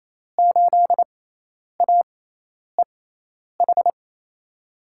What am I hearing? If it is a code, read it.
Morse code, heard as a keyed tone, about 28 wpm, spaced as for 10 wpm, 700 Hz. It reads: 8AEH
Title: USA Call Signs - 5 Characters - 6x Character Spacing - 28wpm